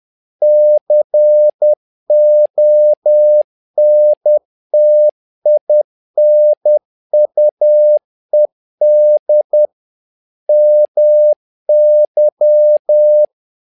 Morse code, heard as a keyed tone, about 10 wpm, 600 Hz. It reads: CONTINUED MY
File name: learning center / tower trouble / tower trouble 10wpm